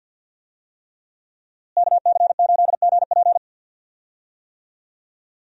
Morse code, heard as a keyed tone, about 52 wpm, 700 Hz. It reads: KC8GP
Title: USA Call Signs - 5 Characters - Repeated 3x - 50wpm